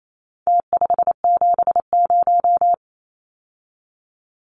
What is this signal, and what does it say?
Morse code, heard as a keyed tone, about 28 wpm, 700 Hz. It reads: T570
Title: Numbers - Contest Style - 4-Digit 28wpm